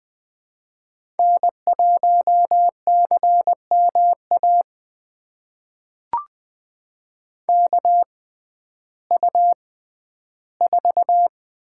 Morse code, heard as a keyed tone, about 21 wpm, spaced as for 12 wpm, 700 Hz. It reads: N1CMA KU4
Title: USA Call Signs - 6x Character Spacing - 20wpm